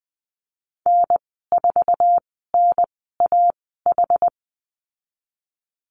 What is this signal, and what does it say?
Morse code, heard as a keyed tone, about 20 wpm, spaced as for 14 wpm, 700 Hz. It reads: N4NAH
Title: 062 Lesson 11g - Letter h - Call Signs 20wpm